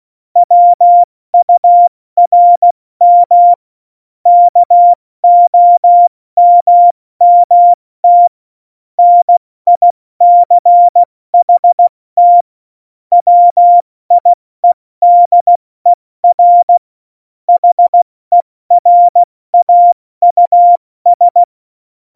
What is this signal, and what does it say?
Morse code, heard as a keyed tone, about 16 wpm, 700 Hz. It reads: WURM KOMMT NICHT WIEDER HERAUS
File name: Morsekurs / Witze / Witz 13 March 2018, 70 F80